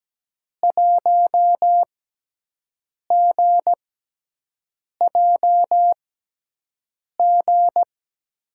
Morse code, heard as a keyed tone, about 17 wpm, spaced as for 6 wpm, 700 Hz. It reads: 1GJG
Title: USA Call Signs - 5 Characters - 6x Character Spacing - 17wpm